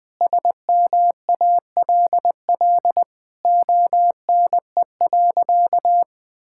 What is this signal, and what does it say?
Morse code, heard as a keyed tone, about 20 wpm, 700 Hz. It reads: SMALL ONE.
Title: Sentences with 2-Word Phrases - Spoken Beginning - 20wpm